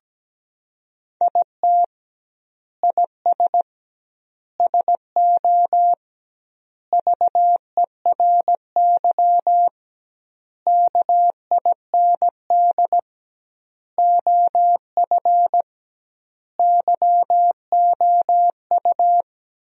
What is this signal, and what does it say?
Morse code, heard as a keyed tone, about 17 wpm, 700 Hz. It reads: IT IS SO VERY KIND OF YOU
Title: 2x Word-Spacing - Sentences from Top 300 Words - 17wpm